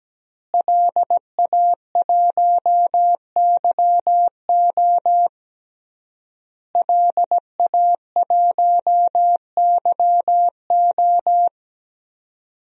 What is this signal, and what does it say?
Morse code, heard as a keyed tone, about 17 wpm, 700 Hz. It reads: LA1YO LA1YO
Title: DXCC-100 Call Signs - 5 Characters - Repeated 3x - 17wpm